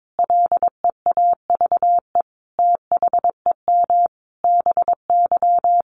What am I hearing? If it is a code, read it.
Morse code, heard as a keyed tone, about 22 wpm, 700 Hz. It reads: LEAVE THEM BY
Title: Sentences from Top 800 Words 22wpm